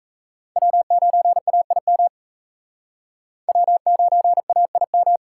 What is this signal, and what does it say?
Morse code, heard as a keyed tone, about 34 wpm, 700 Hz. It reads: W9AIM W9AIM
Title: Speed Racing - USA 1x3 Call Signs 28wpm